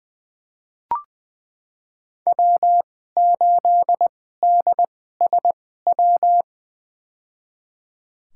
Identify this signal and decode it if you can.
Morse code, heard as a keyed tone, about 20 wpm, spaced as for 14 wpm, 700 Hz. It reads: W8DSW
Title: USA Call Signs - 2x Character Spacing - 20wpm